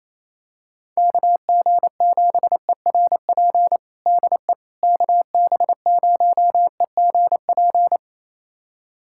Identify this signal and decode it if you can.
Morse code, heard as a keyed tone, about 28 wpm, 700 Hz. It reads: KG7ERP DE KB0EGP
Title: USA Callsign DE Callsign 28wpm